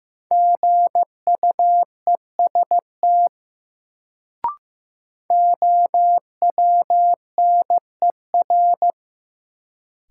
Morse code, heard as a keyed tone, about 15 wpm, 700 Hz. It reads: GUEST OWNER
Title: Sentences from Top 1600 Words - New Word Drill Alt - 15wpm